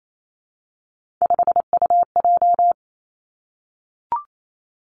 Morse code, heard as a keyed tone, about 28 wpm, 700 Hz. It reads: 5UJ
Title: Three Challenging Letter-Number Combinations 28wpm